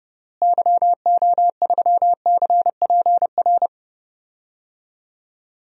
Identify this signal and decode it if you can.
Morse code, heard as a keyed tone, about 30 wpm, 700 Hz. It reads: YO3CPR
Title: DXCC-100 Call Signs - 6 Characters - Repeated 2x - 30wpm